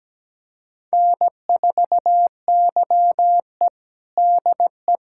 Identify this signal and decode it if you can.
Morse code, heard as a keyed tone, about 17 wpm, 700 Hz. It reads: N4YE DE
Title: USA Callsign DE Callsign - 4 Characters - 17wpm